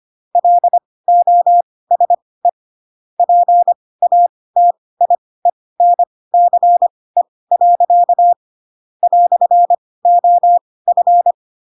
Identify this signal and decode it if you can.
Morse code, heard as a keyed tone, about 25 wpm, spaced as for 18 wpm, 700 Hz. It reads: LOSE PATIENCE. "OF
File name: Pinocchio / Pinocchio ch4 12 25 ik1hge